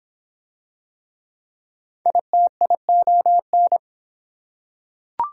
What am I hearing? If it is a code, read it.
Morse code, heard as a keyed tone, about 26 wpm, 700 Hz. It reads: ITION